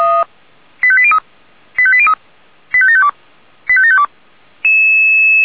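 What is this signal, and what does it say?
Five-tone selective call (ZVEI), heard as quick runs of five tones, tones of about 70 ms. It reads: 85803 85803 85752 85752
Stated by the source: ZVEI five-tone call 85827 received by SDR.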